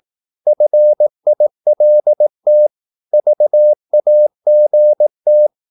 Morse code, heard as a keyed tone, about 18 wpm, 600 Hz. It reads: FILT VAGT
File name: Korte ord DK 18wpm